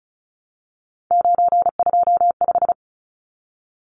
Morse code, heard as a keyed tone, about 34 wpm, 700 Hz. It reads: 925
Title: Numbers - 3-Digit 35wpm